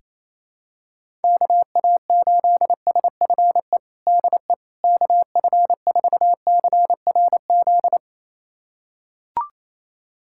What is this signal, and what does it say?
Morse code, heard as a keyed tone, about 28 wpm, 700 Hz. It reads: KA8SFE DE KF4CRZ
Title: USA Callsign DE Callsign - 6 Characters - 28wpm